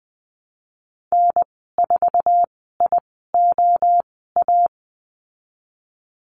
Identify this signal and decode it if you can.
Morse code, heard as a keyed tone, about 20 wpm, spaced as for 15 wpm, 700 Hz. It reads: N4IOA
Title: 048 Lesson 9g - Number 4 - Call Signs 20wpm